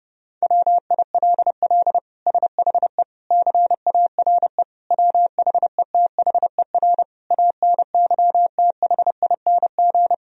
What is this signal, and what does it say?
Morse code, heard as a keyed tone, about 30 wpm, 700 Hz. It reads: WILL SHE CARE WHETHER ANYTHING
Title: Sentences from Top 1000 Words - Review Part 3 - 30wpm